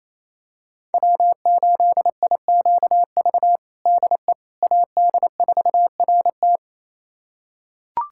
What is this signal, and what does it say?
Morse code, heard as a keyed tone, about 28 wpm, 700 Hz. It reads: W8IQV DE AD4RT
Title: USA Callsign DE Callsign 28wpm